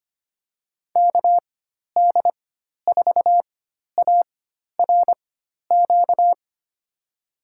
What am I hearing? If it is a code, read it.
Morse code, heard as a keyed tone, about 25 wpm, spaced as for 12 wpm, 700 Hz. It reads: KD4ARQ
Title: USA Call Signs - 6 Characters - 4x Character Spacing - 25wpm